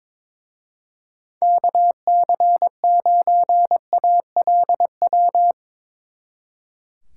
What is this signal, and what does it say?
Morse code, heard as a keyed tone, about 25 wpm, 700 Hz. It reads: KC9ALW
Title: Speed Racing - USA Call Signs 22wpm